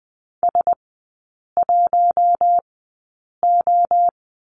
Morse code, heard as a keyed tone, about 20 wpm, spaced as for 8 wpm, 700 Hz. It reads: S1O
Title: 041 Lesson 8e - Number 1 - Three Characters 20wpm